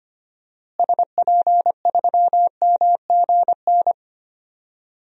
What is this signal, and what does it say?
Morse code, heard as a keyed tone, about 25 wpm, 700 Hz. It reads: SP3MGN
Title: DXCC-100 Call Signs - 6 Characters - Repeated 3x - 25wpm